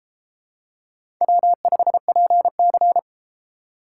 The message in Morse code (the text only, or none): W5PC